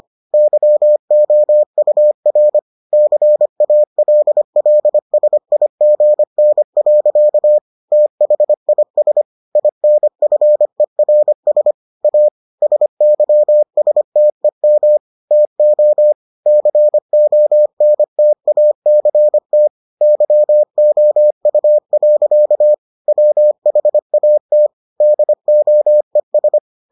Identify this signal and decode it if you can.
Morse code, heard as a keyed tone, about 25 wpm, 600 Hz. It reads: YOUR CALLSIGN. THIS INFERS A SYSTEM TO CONTACT YOU. WHAT DOES